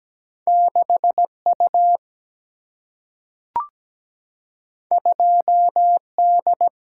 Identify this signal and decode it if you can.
Morse code, heard as a keyed tone, about 17 wpm, 700 Hz. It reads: 6U 2D
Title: Single-Letter Single-Number 17wpm